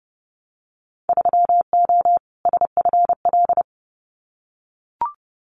Morse code, heard as a keyed tone, about 30 wpm, 700 Hz. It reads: 3O SFL
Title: Winter Field Day Exchanges 30wpm